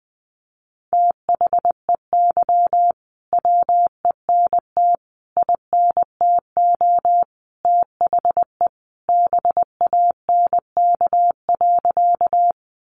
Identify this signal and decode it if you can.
Morse code, heard as a keyed tone, about 20 wpm, 700 Hz. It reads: THEY WENT INTO THE BANK.
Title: Sentences from Top 900 Words 20wpm